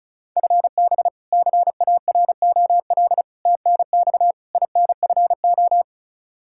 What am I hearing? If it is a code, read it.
Morse code, heard as a keyed tone, about 34 wpm, 700 Hz. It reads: FB CAROL TNX INFO
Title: Practice QSO - Exchange 3 - More Info - 35wpm